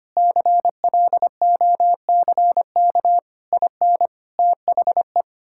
Morse code, heard as a keyed tone, about 25 wpm, 700 Hz. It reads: CLOCK IN THE
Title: Pangram Challenge v2 - 25wpm